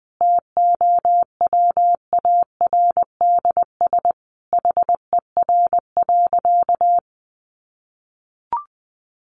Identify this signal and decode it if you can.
Morse code, heard as a keyed tone, about 20 wpm, 700 Hz. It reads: TOWARDS HER.
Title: Sentences from Top 1200 Words 20wpm